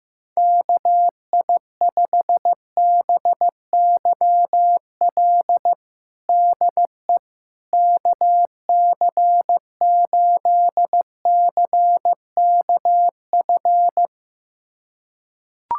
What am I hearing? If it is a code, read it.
Morse code, heard as a keyed tone, about 15 wpm, 700 Hz. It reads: KI5BYL DE KC8CKF
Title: USA Callsign DE Callsign - 6 Characters - 15wpm